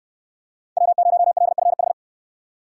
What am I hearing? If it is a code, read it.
Morse code, heard as a keyed tone, about 68 wpm, 700 Hz. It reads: W0LFH